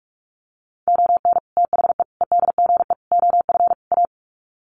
Morse code, heard as a keyed tone, about 44 wpm, 700 Hz. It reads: ON THE EDGE OF A
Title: Sets of 5 Words Encore2 45wpm